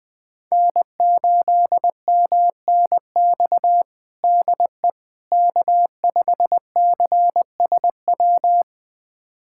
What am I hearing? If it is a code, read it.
Morse code, heard as a keyed tone, about 20 wpm, 700 Hz. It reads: N8MNX DE K5CSW
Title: USA Callsign DE Callsign 20wpm